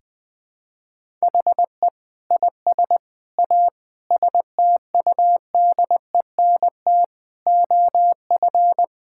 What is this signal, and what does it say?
Morse code, heard as a keyed tone, about 20 wpm, 700 Hz. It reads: HE IS A STUDENT OF
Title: Sentences from Top 1000 Words - Review Part 6 - 20wpm